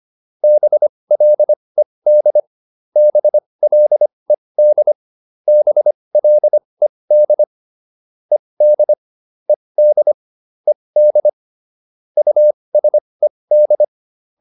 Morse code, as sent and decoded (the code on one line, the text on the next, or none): -... .-.. -.. / -... .-.. -.. / -... .-.. -.. / . -.. / . -.. / . -.. / ..- ... -..
BLED BLED BLED ED ED ED USED